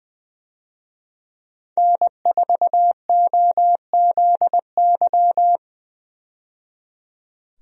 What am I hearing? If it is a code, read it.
Morse code, heard as a keyed tone, about 20 wpm, 700 Hz. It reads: N4OZY
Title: USA Call Signs - 5 Characters - 20wpm